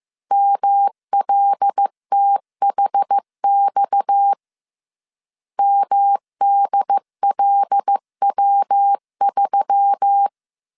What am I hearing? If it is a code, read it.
Morse code, heard as a keyed tone, about 15 wpm, 800 Hz. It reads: MLTHX MDLW3